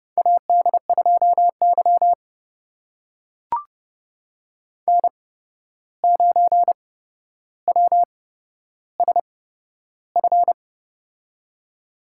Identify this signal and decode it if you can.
Morse code, heard as a keyed tone, about 30 wpm, spaced as for 12 wpm, 700 Hz. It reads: AD2Y N9WSF
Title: USA Call Signs - 8x Character Spacing - 30wpm